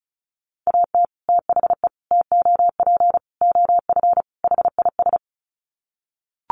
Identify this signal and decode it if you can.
Morse code, heard as a keyed tone, about 34 wpm, 700 Hz. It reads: AT THE TOP OF HIS
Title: Sets of 5 Words Encore 35wpm